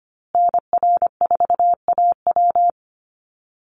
Morse code, heard as a keyed tone, about 25 wpm, 700 Hz. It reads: NR4AW
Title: USA Call Signs Repeated 3x 25wpm